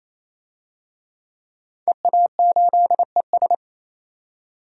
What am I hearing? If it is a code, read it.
Morse code, heard as a keyed tone, about 28 wpm, 700 Hz. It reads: EA8ES
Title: DXCC-100 Call Signs - 5 Characters - 28wpm